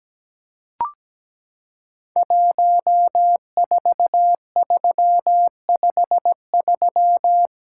Morse code, heard as a keyed tone, about 17 wpm, 700 Hz. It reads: 14353
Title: Code Groups - 5 Letter:Numbers - 17wpm